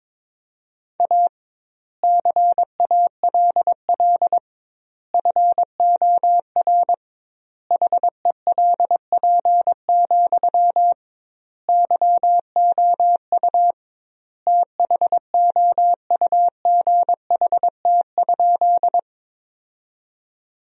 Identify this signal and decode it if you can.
Morse code, heard as a keyed tone, about 22 wpm, 700 Hz. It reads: A CALL FOR HELP, YOU THOUGHT?
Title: 2x Word-Spacing - Sentences from Top 300 Words - 22wpm